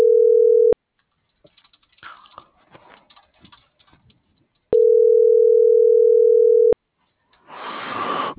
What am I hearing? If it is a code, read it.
Ringback tone (North America), 440 + 480 Hz.